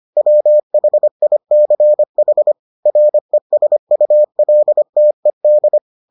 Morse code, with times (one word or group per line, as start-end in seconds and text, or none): 0.2-2.5 WHICH
2.9-5.8 RESULTED